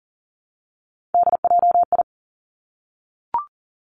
Morse code, heard as a keyed tone, about 41 wpm, 700 Hz. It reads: DJI
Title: Three Challenging Letter-Number Combinations 40wpm